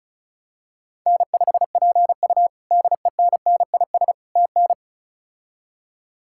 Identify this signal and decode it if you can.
Morse code, heard as a keyed tone, about 36 wpm, 700 Hz. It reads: N5PU DENNIS TN
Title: K1USN SST QSO Practice - 35wpm